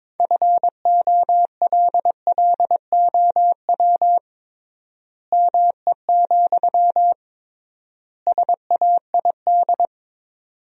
Morse code, heard as a keyed tone, about 22 wpm, 700 Hz. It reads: FOLLOW ME, SAID
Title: 3x Word-Spacing - Sentences from Top 200 Words - 22wpm